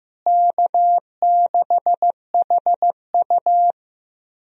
Morse code, heard as a keyed tone, about 15 wpm, 700 Hz. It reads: K6HU
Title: USA Call Signs - 4 Characters - Repeated 3x - 15wpm